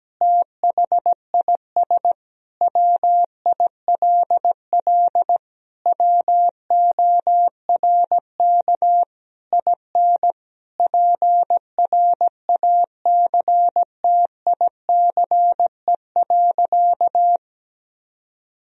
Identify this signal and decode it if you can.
Morse code, heard as a keyed tone, about 17 wpm, 700 Hz. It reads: THIS WILL WORK IN PRACTICE.
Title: Sentences from Top 1000 Words - Review Part 4 - 17wpm